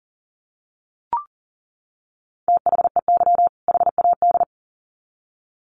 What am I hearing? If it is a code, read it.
Morse code, heard as a keyed tone, about 40 wpm, 700 Hz. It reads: THEY HAD
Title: Sets of 2 Words 40wpm